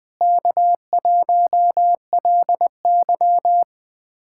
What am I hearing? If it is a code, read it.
Morse code, heard as a keyed tone, about 20 wpm, 700 Hz. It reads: K1LY
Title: USA Callsign DE Callsign - 4 Characters - 20wpm